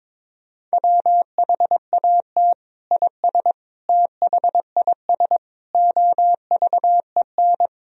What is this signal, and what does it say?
Morse code, heard as a keyed tone, about 22 wpm, 700 Hz. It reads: WHAT IS THIS OVEN
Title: Sentences from Top 1000 Words - Review Part 4 - 22wpm